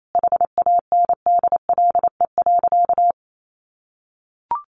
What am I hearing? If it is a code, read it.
Morse code, heard as a keyed tone, about 28 wpm, 700 Hz. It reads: HANDLE.
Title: Sentences from Top 1100 Words 28wpm